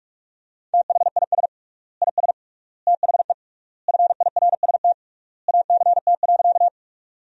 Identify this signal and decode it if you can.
Morse code, heard as a keyed tone, about 46 wpm, 700 Hz. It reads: THIS IS THE FIRST ACT.
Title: 3x Word-Spacing - Sentences from Top 200 Words - 45wpm